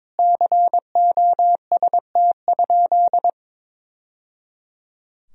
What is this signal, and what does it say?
Morse code, heard as a keyed tone, about 22 wpm, 700 Hz. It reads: COST?
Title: Sentences from Top 1000 Words - Review Part 4 - 22wpm